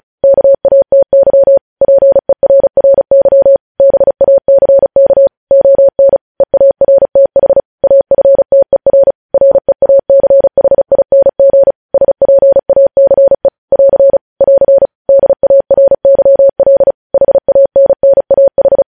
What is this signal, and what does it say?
Morse code, heard as a keyed tone, about 36 wpm, 550 Hz. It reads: KATY PERRY BACK ON EARTH AFTER REACHING SPACE <AR> <AR> DARYL HANNAH
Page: news headlines in morse code at 35 wpm